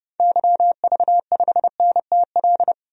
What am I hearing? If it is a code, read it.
Morse code, heard as a keyed tone, about 30 wpm, 700 Hz. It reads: YV5NTL
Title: DXCC-100 Call Signs - 6 Characters - 30wpm